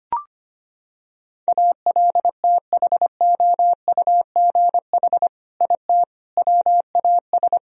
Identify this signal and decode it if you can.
Morse code, heard as a keyed tone, about 25 wpm, 700 Hz. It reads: ALTHOUGH IT WAS
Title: Sentences from Top 1000 Words - Review Part 3 - 25wpm